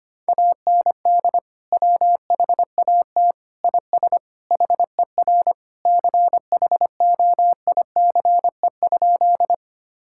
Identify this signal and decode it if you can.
Morse code, heard as a keyed tone, about 25 wpm, 700 Hz. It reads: AND WHAT IS HER CHOICE?
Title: Sentences from Top 1000 Words - Review Part 6 - 25wpm